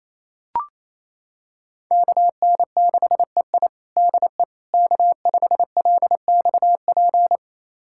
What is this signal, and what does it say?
Morse code, heard as a keyed tone, about 28 wpm, 700 Hz. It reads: KN6EI DE K5LXP
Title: USA Callsign DE Callsign - 5 Characters - 28wpm